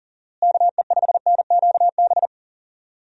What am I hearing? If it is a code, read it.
Morse code, heard as a keyed tone, about 37 wpm, 700 Hz. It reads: KE5NQB
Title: USA Call Signs - 6 Characters - Repeated 2x - 40wpm